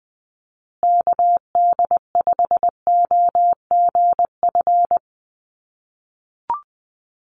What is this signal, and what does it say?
Morse code, heard as a keyed tone, about 20 wpm, 700 Hz. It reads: KD5OGF
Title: USA Call Signs Repeated 2x 20wpm